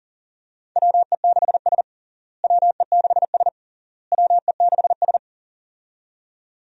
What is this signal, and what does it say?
Morse code, heard as a keyed tone, about 40 wpm, 700 Hz. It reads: WE6S WE6S WE6S
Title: USA Call Signs - 4 Characters - Repeated 3x - 40wpm